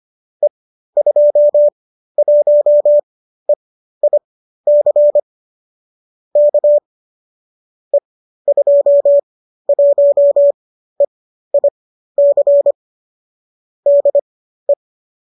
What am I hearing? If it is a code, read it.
Morse code, heard as a keyed tone, about 25 wpm, spaced as for 13 wpm, 590 Hz. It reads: E21EIC K E21EIC DE